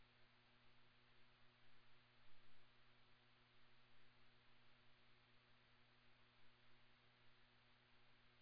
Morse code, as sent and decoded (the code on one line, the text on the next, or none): none